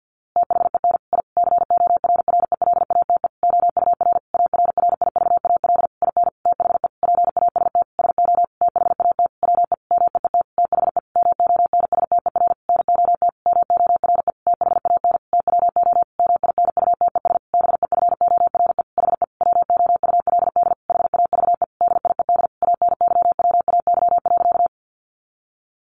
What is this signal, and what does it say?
Morse code, heard as a keyed tone, about 50 wpm, 700 Hz. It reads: THEN I CORRELATE OUR ARRIVAL IN THE PAST SO THAT WE MEET THE MONSTER NOT MORE THAN TWO MINUTES BEFORE HE WOULD HAVE DIED ANYWAY.
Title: A Sound Of Thunder 50wpm